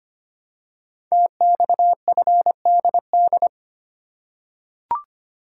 Morse code, heard as a keyed tone, about 25 wpm, 700 Hz. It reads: TXFDD
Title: Code Groups - 5 Letters - 25wpm